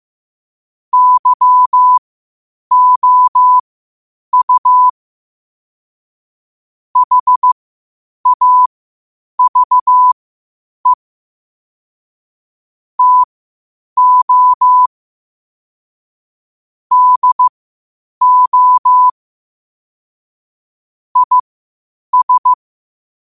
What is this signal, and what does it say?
Morse code, heard as a keyed tone, about 15 wpm, spaced as for 8 wpm, 1000 Hz. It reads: YOU HAVE TO DO IS
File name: qotdfiles / 8wpm